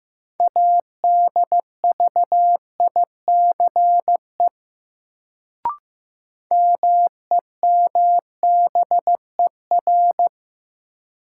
Morse code, heard as a keyed tone, about 15 wpm, 700 Hz. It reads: ADVICE MEMBER
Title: Sentences from Top 1100 Words - New Word Drill - 15wpm